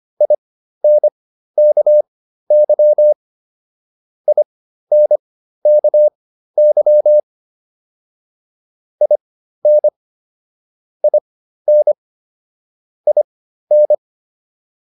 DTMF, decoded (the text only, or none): none